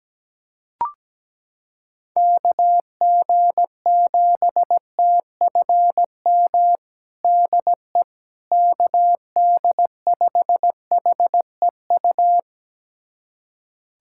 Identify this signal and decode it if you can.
Morse code, heard as a keyed tone, about 17 wpm, 700 Hz. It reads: KG7TFM DE KD5HEU
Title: USA Callsign DE Callsign - 6 Characters - 17wpm